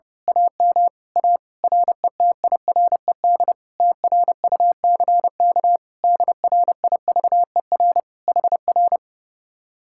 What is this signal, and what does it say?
Morse code, heard as a keyed tone, about 28 wpm, 700 Hz. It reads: AM A RETIRED TRUCK DRIVER HR
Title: Speed-Racing Common QSO Phrases Encore - 30wpm